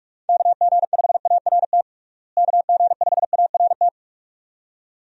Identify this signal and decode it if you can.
Morse code, heard as a keyed tone, about 41 wpm, 700 Hz. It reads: KG5ART KG5ART
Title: USA Call Signs - 6 Characters - Repeated 2x - 45wpm